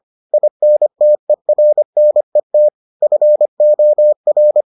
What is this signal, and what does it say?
Morse code, heard as a keyed tone, about 25 wpm, 600 Hz. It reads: INTERNET FOR